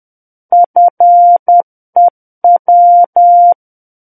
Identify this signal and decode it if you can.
Morse code, heard as a keyed tone, about 10 wpm, 700 Hz. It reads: FEW
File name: Book One / 01-Chapter 1 - 10 WPM